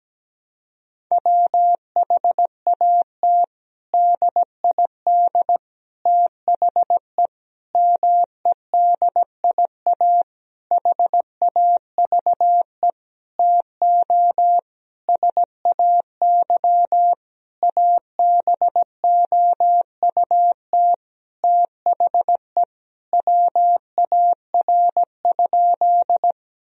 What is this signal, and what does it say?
Morse code, heard as a keyed tone, about 17 wpm, 700 Hz. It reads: WHAT DID THE MEDIA HAVE TO SAY ABOUT THE WAR?
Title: Sentences from Top 1000 Words - Review Part 4 - 17wpm